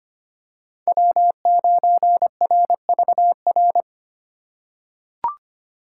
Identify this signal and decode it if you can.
Morse code, heard as a keyed tone, about 25 wpm, 700 Hz. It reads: W9RVR
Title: USA Call Signs - 5 Characters - 4x Character Spacing - 25wpm